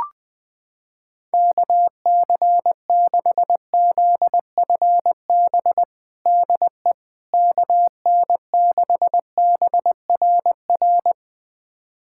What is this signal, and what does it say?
Morse code, heard as a keyed tone, about 20 wpm, 700 Hz. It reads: KC6ZFB DE KN6BRR
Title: USA Callsign DE Callsign 20wpm